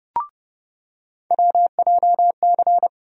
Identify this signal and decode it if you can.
Morse code, heard as a keyed tone, about 30 wpm, 700 Hz. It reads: WJC
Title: USA Call Sign Postfixes - 3 Letters 30wpm